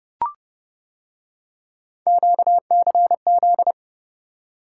Morse code, heard as a keyed tone, about 30 wpm, 700 Hz. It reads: QCZ